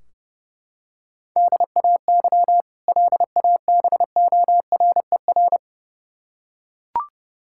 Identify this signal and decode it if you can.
Morse code, heard as a keyed tone, about 30 wpm, 700 Hz. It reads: DAY LABORER
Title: K1USN SST QSO Practice 30wpm